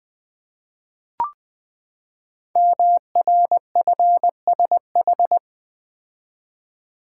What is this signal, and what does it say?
Morse code, heard as a keyed tone, about 20 wpm, 700 Hz. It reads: MRFSH
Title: Code Groups - 5 Letters - 20wpm